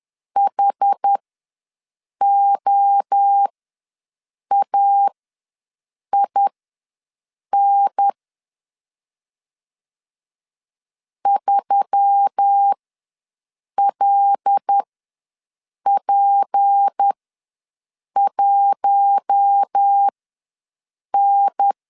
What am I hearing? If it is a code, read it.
Morse code, heard as a keyed tone, about 10 wpm, spaced as for 6 wpm, 800 Hz. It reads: HOAIN 3LP1N
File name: LEZIONI / 6 PKX0 / 6 P ws 10wpm 3751